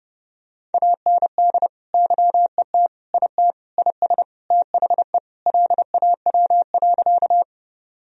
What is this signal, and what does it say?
Morse code, heard as a keyed tone, about 30 wpm, 700 Hz. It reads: AND YET IT IS THE LAW.
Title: Sentences from Top 1000 Words - Review Part 3 - 30wpm